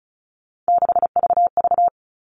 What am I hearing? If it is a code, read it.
Morse code, heard as a keyed tone, about 34 wpm, 700 Hz. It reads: 6VV
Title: All Cut Numbers - 3-Digit 35wpm